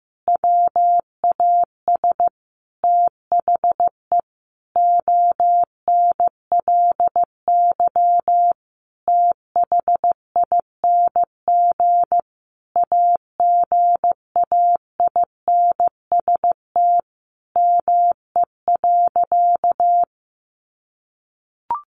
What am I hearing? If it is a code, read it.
Morse code, heard as a keyed tone, about 15 wpm, 700 Hz. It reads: WAS THE ONLY THING AGAINST ME.
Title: Sentences from Top 1100 Words 15wpm